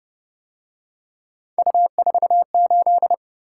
Morse code, heard as a keyed tone, about 30 wpm, 700 Hz. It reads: U48